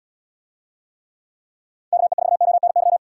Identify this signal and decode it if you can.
Morse code, heard as a keyed tone, about 79 wpm, 700 Hz. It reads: KE4ONY